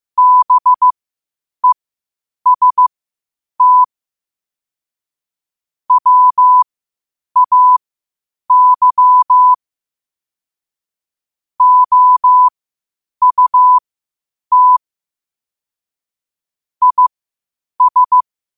Morse code, heard as a keyed tone, about 15 wpm, spaced as for 8 wpm, 1000 Hz. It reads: BEST WAY OUT IS
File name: qotdfiles / 8wpm